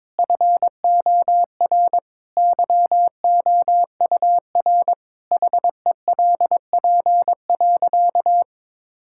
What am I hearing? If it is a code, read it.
Morse code, heard as a keyed tone, about 22 wpm, 700 Hz. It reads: FOR YOUR HELP.
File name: practice / 22wpm / Sentences with 4-Word Phrases - 22wpm